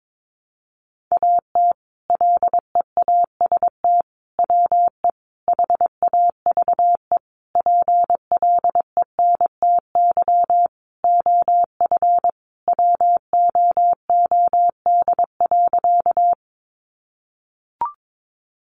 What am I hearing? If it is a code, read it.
Morse code, heard as a keyed tone, about 22 wpm, 700 Hz. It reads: AT LEAST WE HAVE PLENTY OF WOOD.
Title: Sentences from Top 1300 Words 22wpm